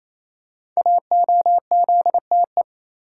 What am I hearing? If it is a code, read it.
Morse code, heard as a keyed tone, about 27 wpm, 700 Hz. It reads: AOZTE
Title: Code Groups - 5 Letters - 28wpm